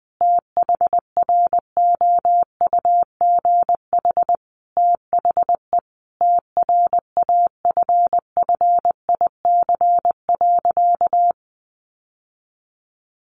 Morse code, heard as a keyed tone, about 20 wpm, 700 Hz. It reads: THROUGH THE TRAFFIC.
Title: Sentences from Top 1300 Words 20wpm